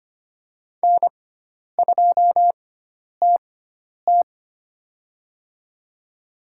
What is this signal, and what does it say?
Morse code, heard as a keyed tone, about 25 wpm, spaced as for 10 wpm, 700 Hz. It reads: N2TT
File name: Lesson 16g - Letter c - Call Signs 25wpm